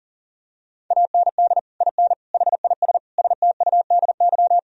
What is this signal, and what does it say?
Morse code, heard as a keyed tone, about 40 wpm, 700 Hz. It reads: AND IN HIS STUDY
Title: Sentences from Top 1000 Words - Review Part 1 - 40wpm